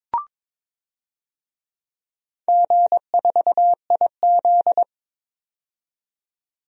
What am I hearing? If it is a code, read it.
Morse code, heard as a keyed tone, about 22 wpm, 700 Hz. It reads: G4IZ
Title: DXCC-100 Call Signs - 4 Characters - 22wpm